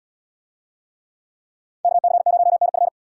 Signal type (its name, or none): Morse code